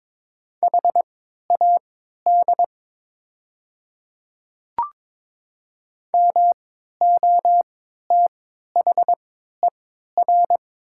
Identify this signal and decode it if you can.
Morse code, heard as a keyed tone, about 23 wpm, spaced as for 13 wpm, 700 Hz. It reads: HAD MOTHER
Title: 3x Character Spacing - Top 200 Words - 22wpm